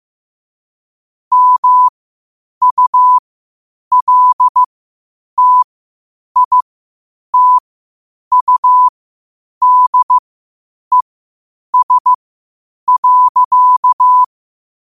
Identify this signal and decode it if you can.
Morse code, heard as a keyed tone, about 15 wpm, spaced as for 8 wpm, 1000 Hz. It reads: MULTITUDES.
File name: qotdfiles / 8wpm